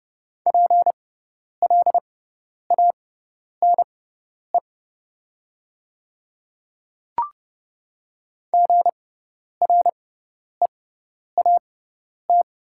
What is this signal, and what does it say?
Morse code, heard as a keyed tone, about 30 wpm, spaced as for 10 wpm, 700 Hz. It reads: PLANE GREAT
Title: 6x Character Spacing - Top 500 Words - 30wpm